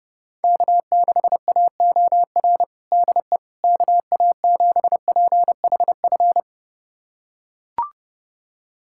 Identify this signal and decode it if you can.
Morse code, heard as a keyed tone, about 32 wpm, 700 Hz. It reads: K6AOR DE KA7PHF
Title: USA Callsign DE Callsign 30wpm